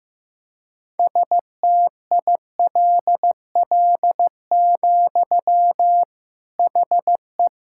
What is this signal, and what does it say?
Morse code, heard as a keyed tone, about 15 wpm, 700 Hz. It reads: STILL, HE
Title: Sentences from Top 1000 Words - Review Part 5 - 15wpm